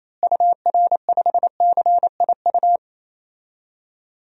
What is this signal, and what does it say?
Morse code, heard as a keyed tone, about 28 wpm, 700 Hz. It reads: UR5CIU
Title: DXCC-100 Call Signs - 6 Characters - Repeated 3x - 28wpm